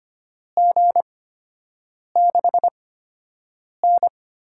Morse code, heard as a keyed tone, about 25 wpm, spaced as for 7 wpm, 700 Hz. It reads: G6N
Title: USA Call Signs - 8x Character Spacing - 25wpm